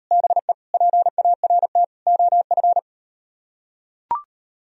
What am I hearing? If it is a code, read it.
Morse code, heard as a keyed tone, about 38 wpm, 700 Hz. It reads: BE PART OF